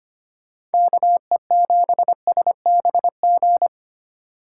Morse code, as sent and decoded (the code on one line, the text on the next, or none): -.- . --... ... -... --.
KE7SBG